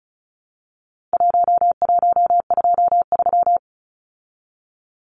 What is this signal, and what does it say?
Morse code, heard as a keyed tone, about 34 wpm, 700 Hz. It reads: 1123
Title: Numbers - 4-Digit 35wpm